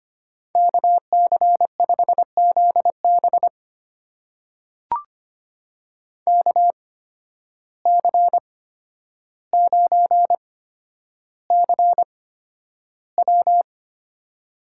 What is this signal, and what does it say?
Morse code, heard as a keyed tone, about 25 wpm, spaced as for 11 wpm, 700 Hz. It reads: KC5ZB KC9CW
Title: USA Call Signs - 5 Characters - 8x Character Spacing - 25wpm